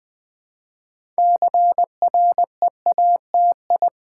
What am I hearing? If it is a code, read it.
Morse code, heard as a keyed tone, about 20 wpm, 700 Hz. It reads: CREATI
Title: Build A Word - 3 Letters At A Time - 8-Letter Words - 20wpm